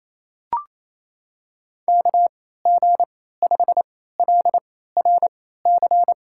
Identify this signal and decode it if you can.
Morse code, heard as a keyed tone, about 28 wpm, spaced as for 16 wpm, 700 Hz. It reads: KG5LRC
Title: USA Call Signs - 3x Character Spacing - 28wpm